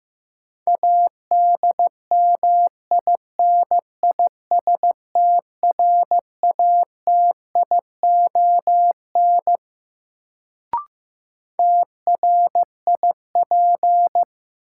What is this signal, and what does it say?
Morse code, heard as a keyed tone, about 15 wpm, 700 Hz. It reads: ADMINISTRATION TRIP
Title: Sentences from Top 1300 Words - New Word Drill Alt - 15wpm